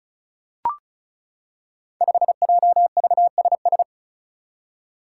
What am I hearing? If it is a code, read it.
Morse code, heard as a keyed tone, about 36 wpm, 700 Hz. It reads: 5JVSS
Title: Five Challenging Letter-Number Combinations 35wpm